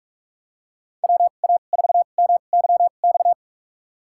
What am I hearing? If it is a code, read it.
Morse code, heard as a keyed tone, about 43 wpm, spaced as for 32 wpm, 700 Hz. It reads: WA4MYX
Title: USA Call Signs - 2x Character Spacing - 45wpm